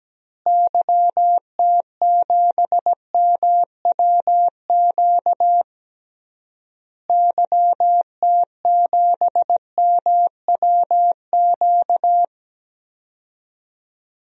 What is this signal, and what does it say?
Morse code, heard as a keyed tone, about 17 wpm, 700 Hz. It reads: YT7MWQ YT7MWQ
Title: DXCC-100 Call Signs - 6 Characters - Repeated 3x - 17wpm